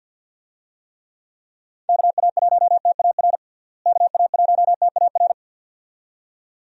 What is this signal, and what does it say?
Morse code, heard as a keyed tone, about 50 wpm, 700 Hz. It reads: KA1TAR KA1TAR
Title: USA Call Signs - 6 Characters - Repeated 2x - 50wpm